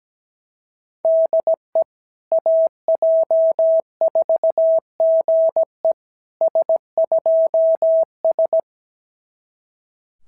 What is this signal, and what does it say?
Morse code, heard as a keyed tone, about 17 wpm, 690 Hz. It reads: DE AJ4GE S2S
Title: SOTA QSO Practice 17wpm